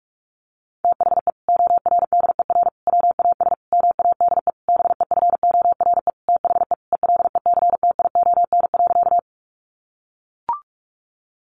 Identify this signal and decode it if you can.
Morse code, heard as a keyed tone, about 44 wpm, 700 Hz. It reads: THE ORDER WAS MADE BEFORE THE ELECTION.
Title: Sentences from Top 1500 Words 45wpm